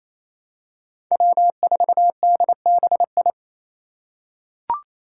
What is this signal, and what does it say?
Morse code, heard as a keyed tone, about 28 wpm, 700 Hz. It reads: W4DBI
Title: Five Challenging Letter-Number Combinations 28wpm